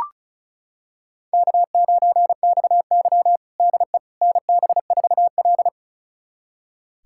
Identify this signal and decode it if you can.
Morse code, heard as a keyed tone, about 36 wpm, 700 Hz. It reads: K9XY DE NB4L
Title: USA Callsign DE Callsign - 4 Characters - 35wpm